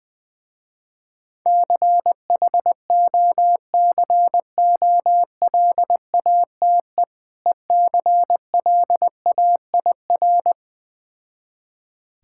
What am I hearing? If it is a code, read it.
Morse code, heard as a keyed tone, about 20 wpm, 700 Hz. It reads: CHOCOLATE ECLAIR
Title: Difficult Words with Hints - 20wpm